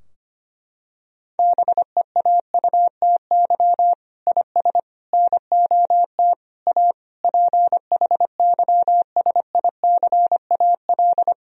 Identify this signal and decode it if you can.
Morse code, heard as a keyed tone, about 25 wpm, 700 Hz. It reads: BEAUTY IS NOT A PHYSICAL
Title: Sentences from Top 1000 Words - Review Part 4 - 25wpm